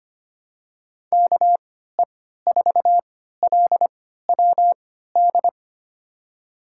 Morse code, heard as a keyed tone, about 25 wpm, spaced as for 14 wpm, 700 Hz. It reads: KE4LWD
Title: USA Call Signs - 6 Characters - 3x Character Spacing - 25wpm